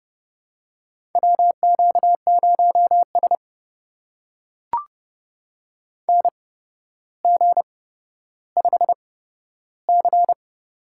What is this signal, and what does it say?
Morse code, heard as a keyed tone, about 30 wpm, spaced as for 13 wpm, 700 Hz. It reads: WQ0S NG5C
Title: USA Call Signs - 4 Characters - 8x Character Spacing - 30wpm